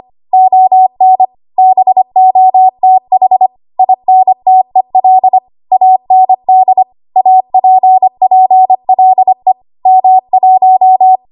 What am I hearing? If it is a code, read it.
Morse code, heard as a keyed tone, about 25 wpm, 750 Hz. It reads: ON BOTH INTEL AND APPLE M1
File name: Archive / 25 WPM / 25WPM